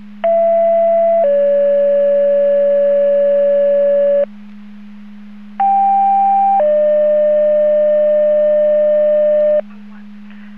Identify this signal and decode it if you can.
Two-tone page, heard as a one-second tone, then a three-second tone, about 670 then 570 Hz, about 790 then 600 Hz. It is heard from 0.2 to 9.6 s.
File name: Portage 154.385 Paging